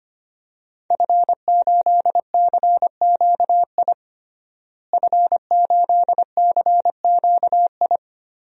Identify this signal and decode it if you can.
Morse code, heard as a keyed tone, about 25 wpm, 700 Hz. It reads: F8CQI F8CQI
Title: DXCC-100 Call Signs - 5 Characters - Repeated 3x - 25wpm